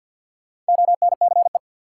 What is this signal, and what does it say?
Morse code, heard as a keyed tone, about 47 wpm, 700 Hz. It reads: ONCE
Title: Sentences from Top 300 Words - New Word Drill - 50wpm